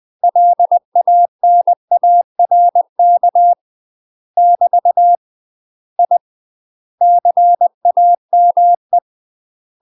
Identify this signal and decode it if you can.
Morse code, heard as a keyed tone, about 20 wpm, 700 Hz. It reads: LANARK <BT> I CAME
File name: UK MASTER QSOs TRAINING/20WPM